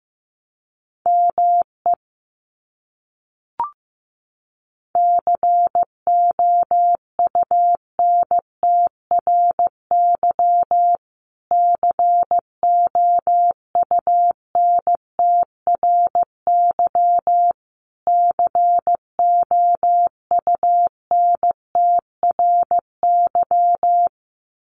Morse code, heard as a keyed tone, about 15 wpm, 700 Hz. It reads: ME COUNTRY COUNTRY COUNTRY
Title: Top 300 Words Repeated 15wpm